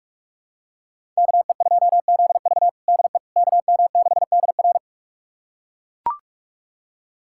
Morse code, heard as a keyed tone, about 42 wpm, 700 Hz. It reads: KE2ZV DE KM6DR
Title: USA Callsign DE Callsign - 5 Characters - 45wpm